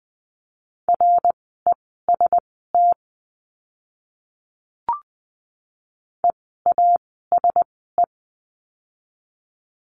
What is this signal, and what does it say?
Morse code, heard as a keyed tone, about 20 wpm, spaced as for 14 wpm, 700 Hz. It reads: REST EASE
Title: 054 Lesson 10f - Letter r - Words 20wpm